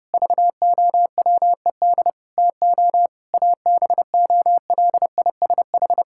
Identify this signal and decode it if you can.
Morse code, heard as a keyed tone, about 30 wpm, 700 Hz. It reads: VOWED TO ABOLISH
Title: Pangram Challenge v2 - 30wpm